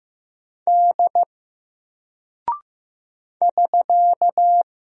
Morse code, heard as a keyed tone, about 15 wpm, 700 Hz. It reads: D <SK>
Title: Single Character (Letter:Number:Punctuation:Prosign) 15wpm